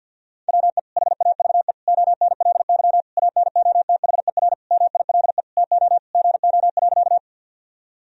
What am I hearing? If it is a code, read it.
Morse code, heard as a keyed tone, about 50 wpm, 700 Hz. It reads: WE HAVE ONLY ANOTHER MILE TO GO.
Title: Sentences from Top 1000 Words - Review Part 1 - 50wpm